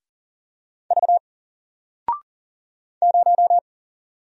Morse code, heard as a keyed tone, about 40 wpm, 700 Hz. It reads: V 0
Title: Single Letter-Number 40wpm